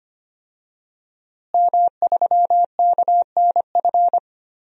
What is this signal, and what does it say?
Morse code, heard as a keyed tone, about 25 wpm, 700 Hz. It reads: M3KNF
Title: DXCC-100 Call Signs - 5 Characters - Repeated 2x - 25wpm